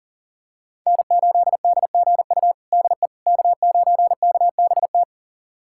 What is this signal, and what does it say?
Morse code, heard as a keyed tone, about 40 wpm, 700 Hz. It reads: N8DGU DE K9KBT
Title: USA Callsign DE Callsign - 5 Characters - 40wpm